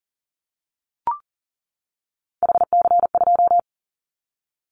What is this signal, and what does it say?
Morse code, heard as a keyed tone, about 39 wpm, 700 Hz. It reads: HC2
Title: Callsign Prefixes for Top 100 DXCC - 40wpm